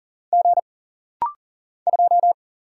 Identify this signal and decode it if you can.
Morse code, heard as a keyed tone, about 41 wpm, 700 Hz. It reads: G 2
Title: Single Letter-Number Alt - Rapid-Fire 40wpm